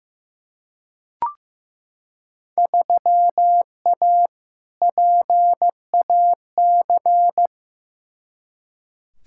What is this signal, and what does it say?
Morse code, heard as a keyed tone, about 15 wpm, 700 Hz. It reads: 3A PAC
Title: Field Day Exchanges - 15wpm